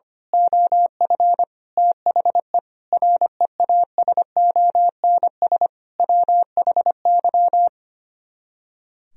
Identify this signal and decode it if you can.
Morse code, heard as a keyed tone, about 24 wpm, 700 Hz. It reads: OF THE REASONS WHY
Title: Speed-Racing - Sets of 5 Words 25wpm